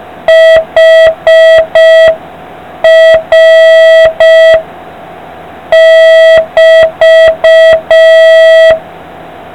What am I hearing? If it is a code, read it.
Morse code, heard as a keyed tone, about 5 wpm, spaced as for 4 wpm, 1950 Hz. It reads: HR<BT>